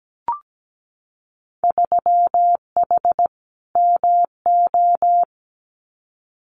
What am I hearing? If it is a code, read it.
Morse code, heard as a keyed tone, about 17 wpm, 700 Hz. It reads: 3H MO
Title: Winter Field Day Exchanges Encore 17wpm